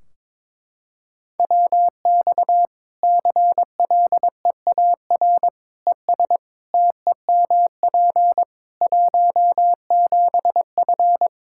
Morse code, heard as a keyed tone, about 22 wpm, 700 Hz. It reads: WX CLEAR ES TEMP 17F
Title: Practice QSO - Exchange 2 - Info - 22wpm